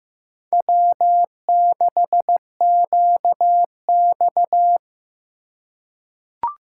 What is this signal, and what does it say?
Morse code, heard as a keyed tone, about 15 wpm, 700 Hz. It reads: W6QX
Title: USA Call Signs - 4 Characters - 2x Character Spacing - 15wpm